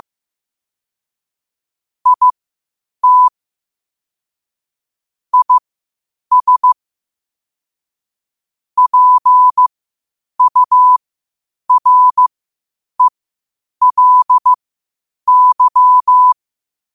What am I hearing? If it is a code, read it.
Morse code, heard as a keyed tone, about 15 wpm, spaced as for 8 wpm, 1000 Hz. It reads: IT IS PURELY